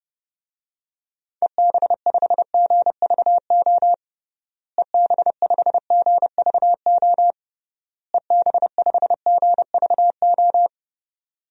Morse code, heard as a keyed tone, about 30 wpm, 700 Hz. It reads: EB5GVO EB5GVO EB5GVO
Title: DXCC-100 Call Signs - 6 Characters - Repeated 3x - 30wpm